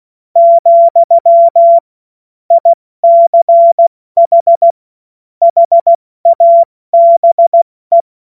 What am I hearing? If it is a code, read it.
Morse code, heard as a keyed tone, about 16 wpm, 680 Hz. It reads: , ICH HABE
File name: Morsekurs / Witze / Witz 23. Dezember 2025 70 F80